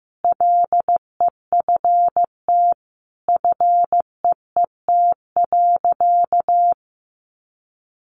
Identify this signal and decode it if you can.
Morse code, heard as a keyed tone, about 15 wpm, 700 Hz. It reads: LEFT FEET.
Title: Sentences from Top 300 Words 15wpm